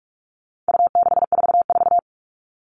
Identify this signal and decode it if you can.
Morse code, heard as a keyed tone, about 44 wpm, 700 Hz. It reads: U644
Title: All Cut Numbers - 4-Digit 45wpm